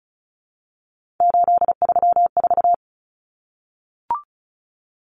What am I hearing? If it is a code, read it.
Morse code, heard as a keyed tone, about 34 wpm, 700 Hz. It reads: 834
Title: Common Cut Numbers - 3-Digit 35wpm